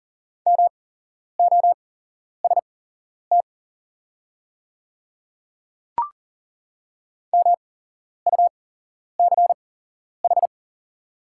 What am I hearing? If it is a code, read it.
Morse code, heard as a keyed tone, about 40 wpm, spaced as for 11 wpm, 700 Hz. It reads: MOST MUCH
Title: 8x Character Spacing - Top 200 Words - 40wpm